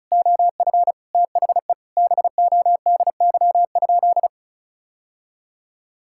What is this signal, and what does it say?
Morse code, heard as a keyed tone, about 36 wpm, 700 Hz. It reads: OF THE BODY?
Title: Sentences from Top 1000 Words - Review Part 6 - 35wpm